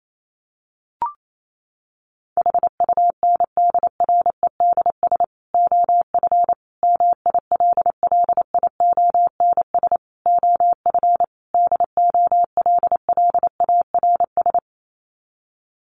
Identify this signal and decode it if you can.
Morse code, heard as a keyed tone, about 28 wpm, 700 Hz. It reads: HUNDREDS OF MILLIONS OF DOLLARS
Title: Sets of 5 Words 28wpm